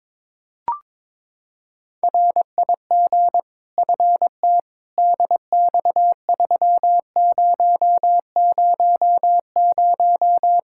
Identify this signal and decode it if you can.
Morse code, heard as a keyed tone, about 21 wpm, 700 Hz. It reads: RIG FT DX3000
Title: K1USN SST QSO Practice 22wpm